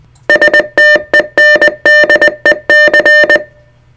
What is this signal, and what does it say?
Morse code, heard as a keyed tone, about 20 wpm, 1800 Hz, with noise. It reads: STENDEC
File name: STENDEC 600hz 20WPM